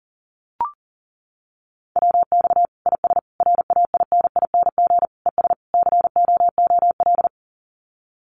Morse code, heard as a keyed tone, about 40 wpm, 700 Hz. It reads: WX IS RAINING ES COOL
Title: Common QSO Phrases 40wpm